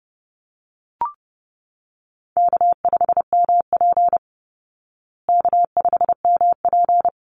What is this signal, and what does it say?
Morse code, heard as a keyed tone, about 30 wpm, 700 Hz. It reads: K5MP K5MP
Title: Easy Call Signs Repeated 3x - 30wpm